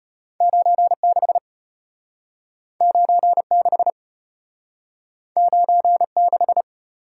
Morse code, heard as a keyed tone, about 34 wpm, 700 Hz. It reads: 96 96 96